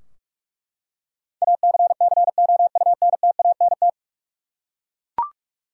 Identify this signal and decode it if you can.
Morse code, heard as a keyed tone, about 43 wpm, 700 Hz. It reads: ACCOUNTANT
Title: K1USN SST QSO Practice 45wpm